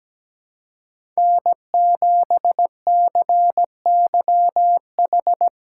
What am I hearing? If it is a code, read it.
Morse code, heard as a keyed tone, about 17 wpm, 700 Hz. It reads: N7CYH
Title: USA Call Signs - 5 Characters - 2x Character Spacing - 17wpm